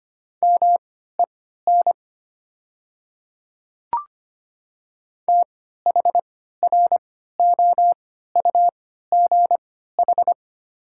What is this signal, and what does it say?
Morse code, heard as a keyed tone, about 25 wpm, spaced as for 14 wpm, 700 Hz. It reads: MEN THROUGH
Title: 3x Character Spacing - Top 300 Words - 25wpm